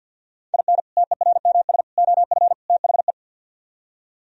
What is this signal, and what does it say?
Morse code, heard as a keyed tone, about 59 wpm, 690 Hz. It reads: IN TERMS OF THE
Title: Speed-Racing - Sets of 4 Words 50wpm